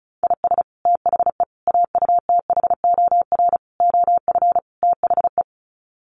Morse code, heard as a keyed tone, about 34 wpm, 700 Hz. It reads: IS THE AUTHOR OF THE
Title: Sets of 5 Words Encore 35wpm